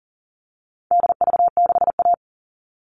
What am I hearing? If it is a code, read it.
Morse code, heard as a keyed tone, about 40 wpm, 700 Hz. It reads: DV6A
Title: All Cut Numbers - 4-Digit 40wpm